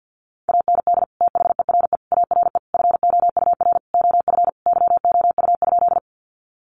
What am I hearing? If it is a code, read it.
Morse code, heard as a keyed tone, about 50 wpm, 700 Hz. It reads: AND THERE ARE FOUR OF YOU?
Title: Sentences from Top 300 Words 50wpm